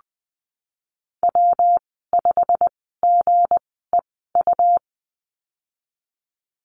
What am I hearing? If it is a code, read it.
Morse code, heard as a keyed tone, about 20 wpm, spaced as for 14 wpm, 700 Hz. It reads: W5GEU
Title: 161 Lesson 26g - Letter g - Call Signs 20wpm